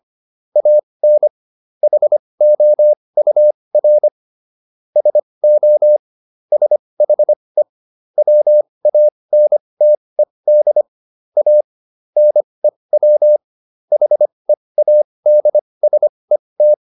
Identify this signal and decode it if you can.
Morse code, heard as a keyed tone, about 25 wpm, spaced as for 20 wpm, 600 Hz. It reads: AN HOUR SO SHE WANTED A NEW HEADSET